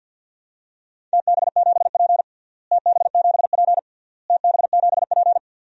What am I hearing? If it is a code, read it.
Morse code, heard as a keyed tone, about 51 wpm, 700 Hz. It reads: TB7P TB7P TB7P
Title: DXCC-100 Call Signs - 4 Characters - Repeated 3x - 50wpm